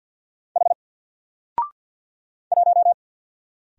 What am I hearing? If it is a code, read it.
Morse code, heard as a keyed tone, about 50 wpm, 700 Hz. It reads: H 1
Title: Single Character (Letter:Number:Punctuation:Prosign) 50wpm